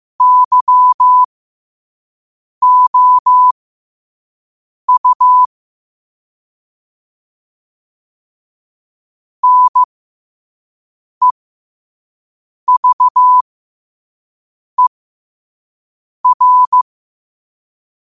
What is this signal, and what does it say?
Morse code, heard as a keyed tone, about 15 wpm, spaced as for 5 wpm, 1000 Hz. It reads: YOU NEVER
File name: qotdfiles / 5wpm